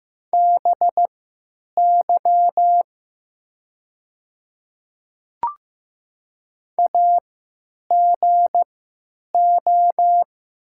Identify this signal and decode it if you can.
Morse code, heard as a keyed tone, about 15 wpm, spaced as for 8 wpm, 700 Hz. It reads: BY AGO
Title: 3x Character Spacing - Top 500 Words - 15wpm